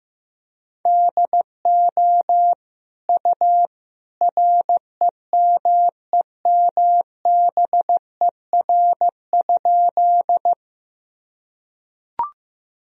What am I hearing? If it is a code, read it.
Morse code, heard as a keyed tone, about 15 wpm, 700 Hz. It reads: DO U REMEMBER?
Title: Sentences from Top 400 Words and Abbreviations 15wpm